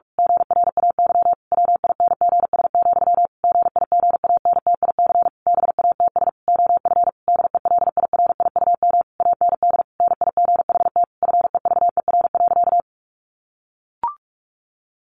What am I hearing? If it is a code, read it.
Morse code, heard as a keyed tone, about 44 wpm, 700 Hz. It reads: GRAY WINGS, GIGANTIC BATS OF DELIRIUM AND NIGHT FEVER.
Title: A Sound Of Thunder 45wpm